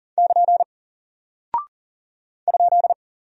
Morse code, heard as a keyed tone, about 40 wpm, 700 Hz. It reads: <KN> ?
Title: Single Character (Letter:Number:Punctuation:Prosign) Alt 40wpm